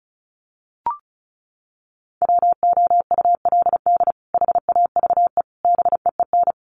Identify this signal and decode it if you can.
Morse code, heard as a keyed tone, about 34 wpm, 700 Hz. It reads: WOULD HAVE BEEN
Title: Sets of 3 Words 35wpm